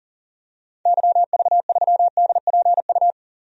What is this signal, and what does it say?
Morse code, heard as a keyed tone, about 40 wpm, 700 Hz. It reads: YV3DPU
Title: DXCC-100 Call Signs - 6 Characters - 40wpm